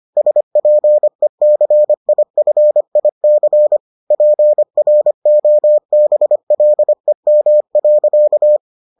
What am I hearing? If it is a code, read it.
Morse code, heard as a keyed tone, about 25 wpm, 600 Hz. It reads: SPECIFIC PROBLEM.